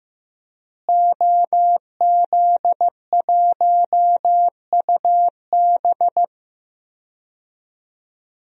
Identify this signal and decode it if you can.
Morse code, heard as a keyed tone, about 15 wpm, 700 Hz. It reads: OZ1UB
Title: DXCC-100 Call Signs - 5 Characters - Repeated 2x - 15wpm